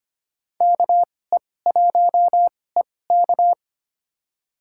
Morse code, heard as a keyed tone, about 25 wpm, spaced as for 18 wpm, 700 Hz. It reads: KE1EK
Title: USA Call Signs - 5 Characters - 2x Character Spacing - 25wpm